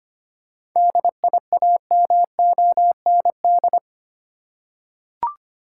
Morse code, heard as a keyed tone, about 25 wpm, 700 Hz. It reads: DIAMOND
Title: Seven Letter Words - 25wpm